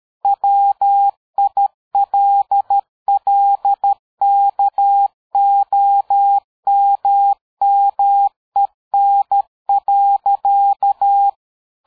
Morse code, heard as a keyed tone, about 13 wpm, 800 Hz. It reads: WILLKOMMEN.
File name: dk5ke / PersWorte 60